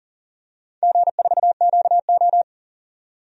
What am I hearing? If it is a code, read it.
Morse code, heard as a keyed tone, about 40 wpm, 700 Hz. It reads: G4QO
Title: DXCC-100 Call Signs - 4 Characters - 40wpm